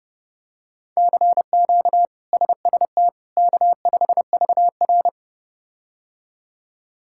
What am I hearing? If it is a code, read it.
Morse code, heard as a keyed tone, about 30 wpm, 700 Hz. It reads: CQ SST K5VR
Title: K1USN SST QSO Practice - 30wpm